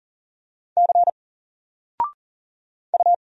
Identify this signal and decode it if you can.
Morse code, heard as a keyed tone, about 40 wpm, 700 Hz. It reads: C U
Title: Single Letter-Number Alt 40wpm